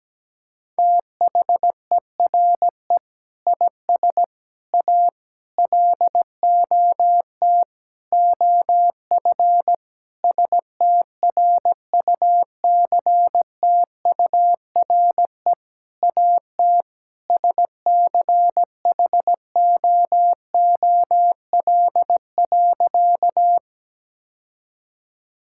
Morse code, heard as a keyed tone, about 17 wpm, 700 Hz. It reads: THERE IS A LOT OF STRUCTURE AT SCHOOL.
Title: Sentences from Top 1000 Words - Review Part 6 - 17wpm